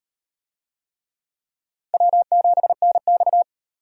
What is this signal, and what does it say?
Morse code, heard as a keyed tone, about 35 wpm, 700 Hz. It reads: W7NX